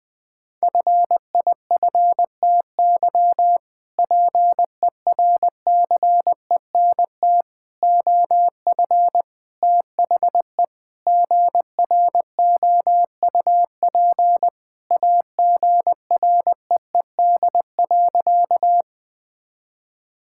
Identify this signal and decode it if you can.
Morse code, heard as a keyed tone, about 21 wpm, 700 Hz. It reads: FIFTY PERCENT OF THE GROUP AGREED.
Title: Sentences with 3-Word Phrases - 20wpm